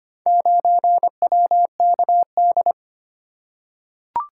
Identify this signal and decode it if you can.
Morse code, heard as a keyed tone, about 25 wpm, 700 Hz. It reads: 9WKD
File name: practice / 25wpm / Four Letter-Number Combinations - 25wpm